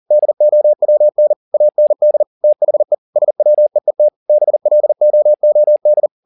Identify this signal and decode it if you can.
Morse code, heard as a keyed tone, about 40 wpm, 600 Hz. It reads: DOWN AND THE SWEET BLOOD
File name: the old man and the sea 1 40wpm